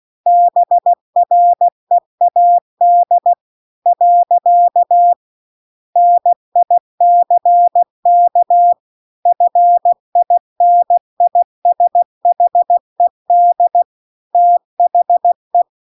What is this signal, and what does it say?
Morse code, heard as a keyed tone, about 16 wpm, 700 Hz. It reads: BREAD . NICK FINISHED THE